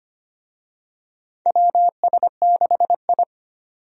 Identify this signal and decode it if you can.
Morse code, heard as a keyed tone, about 25 wpm, 700 Hz. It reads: WS6I
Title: USA Call Signs - 4 Characters - 25wpm